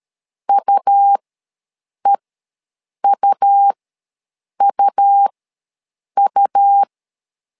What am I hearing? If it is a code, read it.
Morse code, heard as a keyed tone, about 13 wpm, spaced as for 7 wpm, 800 Hz. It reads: UEUUU
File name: LEZIONI / 2 UHDN / 2 U ws 12wpm 1655